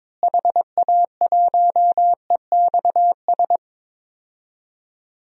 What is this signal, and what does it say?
Morse code, heard as a keyed tone, about 22 wpm, 700 Hz. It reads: HA1EXS
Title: DXCC-100 Call Signs - 6 Characters - Repeated 3x - 22wpm